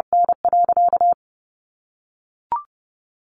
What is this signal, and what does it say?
Morse code, heard as a keyed tone, about 30 wpm, 700 Hz. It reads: N.
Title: Sentences from Top 1200 Words 30wpm